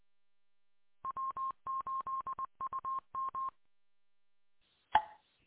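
Morse code, heard as a keyed tone, about 20 wpm, 1060 Hz. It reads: W8UM E
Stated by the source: Recorded from W8UM on 145.230 MHz.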